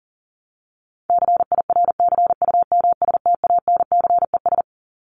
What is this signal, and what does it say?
Morse code, heard as a keyed tone, about 40 wpm, 700 Hz. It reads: CIRCUMSTANCES
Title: ES:S Suffix - Top 100 Words with Base Words 40wpm